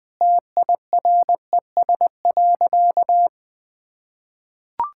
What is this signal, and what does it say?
Morse code, heard as a keyed tone, about 21 wpm, 700 Hz. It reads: TIRES.
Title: Easy Sentences with Hints - 20wpm